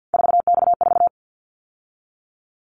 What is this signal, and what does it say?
Morse code, heard as a keyed tone, about 50 wpm, 700 Hz. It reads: 4X4
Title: Callsign Prefixes for Top 50 DXCC - 50wpm